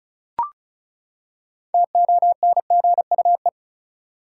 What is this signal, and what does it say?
Morse code, heard as a keyed tone, about 36 wpm, 700 Hz. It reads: TONGUE
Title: Six Letter Words - 35wpm